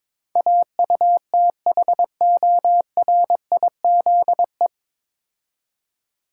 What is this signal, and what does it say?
Morse code, heard as a keyed tone, about 22 wpm, 700 Hz. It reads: AUTHORIZE
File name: 9 Letter Words - First 3 Letters Given - 22wpm